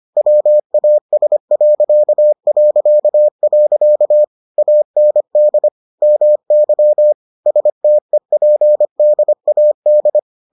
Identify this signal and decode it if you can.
Morse code, heard as a keyed tone, about 25 wpm, 600 Hz. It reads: WAS... AND MY STEPDAD